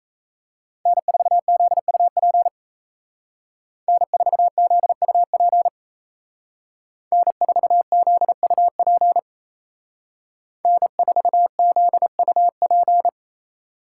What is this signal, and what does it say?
Morse code, heard as a keyed tone, about 35 wpm, 700 Hz. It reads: N4ZUP N4ZUP N4ZUP N4ZUP
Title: Speed Racing - USA 1x3 Call Signs 28wpm